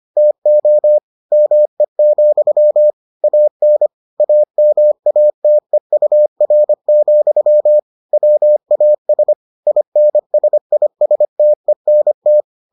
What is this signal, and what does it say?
Morse code, heard as a keyed tone, about 25 wpm, 600 Hz. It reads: TO ME, AN AMATEUR, WAS INSISTENT